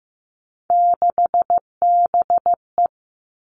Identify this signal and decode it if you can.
Morse code, heard as a keyed tone, about 15 wpm, 700 Hz. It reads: 6BE
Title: Three Challenging Letter-Number Combinations 15wpm